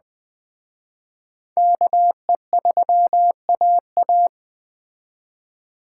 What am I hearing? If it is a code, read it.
Morse code, heard as a keyed tone, about 20 wpm, 700 Hz. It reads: KE3AA
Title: Speed Racing - USA Call Signs 17wpm